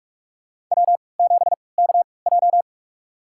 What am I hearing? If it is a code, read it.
Morse code, heard as a keyed tone, about 48 wpm, spaced as for 26 wpm, 700 Hz. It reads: W7KJ
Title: USA Call Signs - 3x Character Spacing - 45wpm